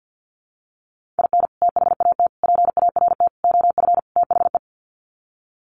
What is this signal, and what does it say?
Morse code, heard as a keyed tone, about 48 wpm, 700 Hz. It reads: IN THAT PART OF THE
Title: Sets of 5 Words Encore2 50wpm